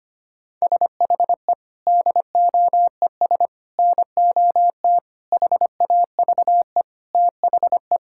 Morse code, heard as a keyed tone, about 25 wpm, 700 Hz. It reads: SHE DOES NOT HAVE THE
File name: practice / 25wpm / Sentences with 3-Word Phrases - 25wpm